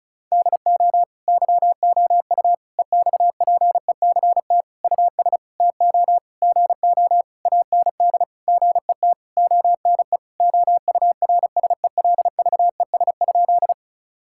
Morse code, heard as a keyed tone, about 34 wpm, 700 Hz. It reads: DO YOU EXPECT US TO GO AND GET ONE OURSELVES?
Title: Sentences from Top 1600 Words 35wpm